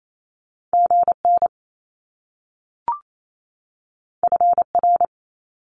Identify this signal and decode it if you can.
Morse code, heard as a keyed tone, about 28 wpm, 700 Hz. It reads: GN FR
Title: Top Two Letter Combinations in Words 28wpm